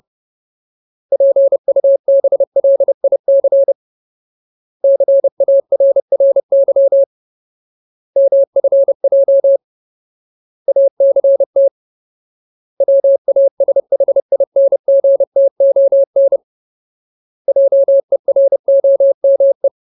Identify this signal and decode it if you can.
Morse code, heard as a keyed tone, about 30 wpm, 550 Hz. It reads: PUBLIC CARRY MFJ ACT WASHINGTON JEROME
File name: Ultimate Word List / 30WPM Ultimate